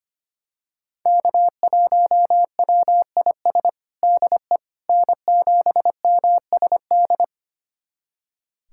Morse code, heard as a keyed tone, about 25 wpm, 700 Hz. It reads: K1WIS DE N7MSD
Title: USA Callsign DE Callsign 25wpm